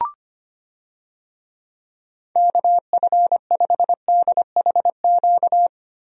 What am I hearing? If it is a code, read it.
Morse code, heard as a keyed tone, about 25 wpm, 700 Hz. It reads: KF5DHQ
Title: USA Call Signs - 6 Characters - Repeated 3x - 25wpm